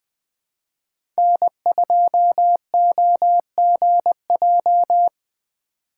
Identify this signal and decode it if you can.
Morse code, heard as a keyed tone, about 20 wpm, 700 Hz. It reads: N2OGJ
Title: USA Call Signs - 5 Characters - 6x Character Spacing - 20wpm